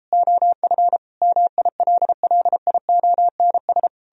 Morse code, heard as a keyed tone, about 38 wpm, 700 Hz. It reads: OF MILLIONS